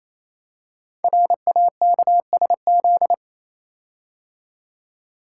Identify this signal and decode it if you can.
Morse code, heard as a keyed tone, about 27 wpm, 700 Hz. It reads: RAKSZ
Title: Code Groups - 5 Letter:Numbers - 28wpm